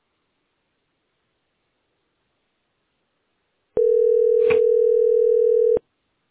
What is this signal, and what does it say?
Ringback tone (North America), 440 + 480 Hz.